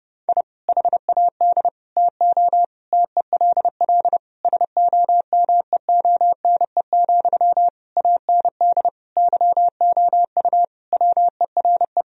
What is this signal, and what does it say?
Morse code, heard as a keyed tone, about 32 wpm, 700 Hz. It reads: I HAD TO TELL SOMEONE, AND YOU WERE
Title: Sentences from Top 1000 Words - Review Part 4 - 30wpm